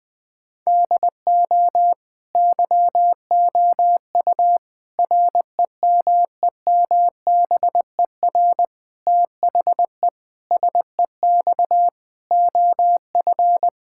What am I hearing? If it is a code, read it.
Morse code, heard as a keyed tone, about 21 wpm, 700 Hz. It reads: DO YOU REMEMBER THE SEX OF
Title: Sentences from Top 1600 Words 20wpm